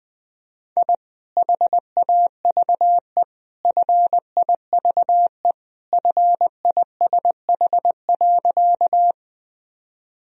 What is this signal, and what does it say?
Morse code, heard as a keyed tone, about 20 wpm, 700 Hz. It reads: I HAVE FIVE FISH.
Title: Sentences from Top 1000 Words - Review Part 2 - 20wpm